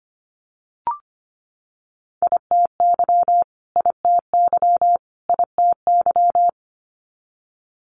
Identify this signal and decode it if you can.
Morse code, heard as a keyed tone, about 25 wpm, 700 Hz. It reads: ITY ITY ITY
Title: Multi-letter Phonemes Repeated 3x 25wpm